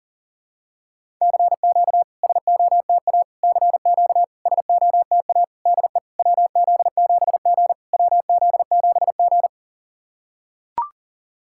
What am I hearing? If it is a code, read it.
Morse code, heard as a keyed tone, about 40 wpm, 700 Hz. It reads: CQ SOTA CQ SOTA DE WZ7G WZ7G
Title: SOTA QSO Practice 40wpm